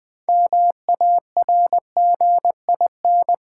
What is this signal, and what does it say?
Morse code, heard as a keyed tone, about 20 wpm, 700 Hz. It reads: MARGIN
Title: 6 Letter Words - First 3 Letters Given - 20wpm